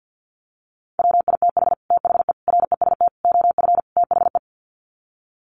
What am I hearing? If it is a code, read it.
Morse code, heard as a keyed tone, about 50 wpm, 700 Hz. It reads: WITH THE REST OF THE
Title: Sets of 5 Words 50wpm